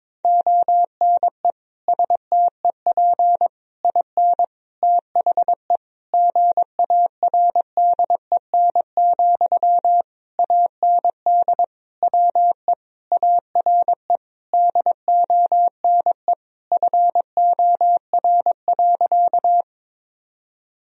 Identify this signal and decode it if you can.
Morse code, heard as a keyed tone, about 22 wpm, 700 Hz. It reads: ONE STEP IN THE GARDEN, AND WE ARE DONE FOR.
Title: Sentences from Top 1000 Words - Review Part 4 - 22wpm